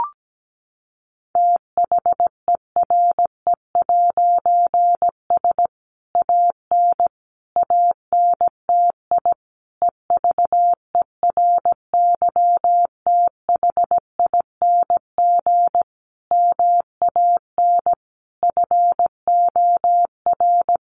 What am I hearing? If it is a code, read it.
Morse code, heard as a keyed tone, about 17 wpm, 700 Hz. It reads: THERE'S AN ANTI EVERYTHING MAN FOR
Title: A Sound Of Thunder 17wpm